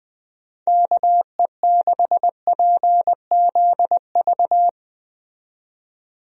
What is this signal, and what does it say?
Morse code, heard as a keyed tone, about 21 wpm, 700 Hz. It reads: KE6PZV
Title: USA Call Signs - 6 Characters - 20wpm